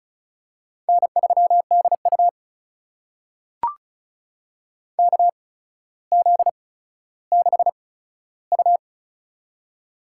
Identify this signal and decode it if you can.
Morse code, heard as a keyed tone, about 36 wpm, spaced as for 15 wpm, 700 Hz. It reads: N3DU KZ6U
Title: USA Call Signs - 4 Characters - 8x Character Spacing - 35wpm